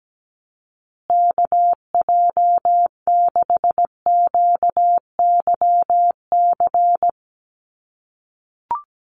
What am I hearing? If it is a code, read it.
Morse code, heard as a keyed tone, about 17 wpm, 700 Hz. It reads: KJ6QYC
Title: USA Call Signs Repeated 2x 17wpm